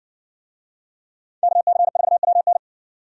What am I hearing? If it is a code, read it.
Morse code, heard as a keyed tone, about 60 wpm, 700 Hz. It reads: KC4WN